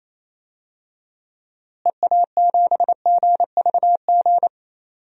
Morse code, heard as a keyed tone, about 28 wpm, 700 Hz. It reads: EA7GVG
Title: DXCC-100 Call Signs - 6 Characters - Repeated 3x - 28wpm